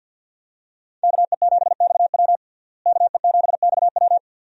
Morse code, heard as a keyed tone, about 52 wpm, 700 Hz. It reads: KE7XW KE7XW
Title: USA Call Signs - 5 Characters - Repeated 2x - 50wpm